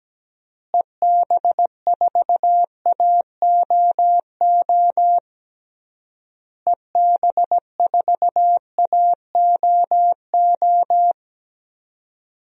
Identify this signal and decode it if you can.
Morse code, heard as a keyed tone, about 17 wpm, 700 Hz. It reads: EB4AOO EB4AOO
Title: DXCC-100 Call Signs - 6 Characters - Repeated 3x - 17wpm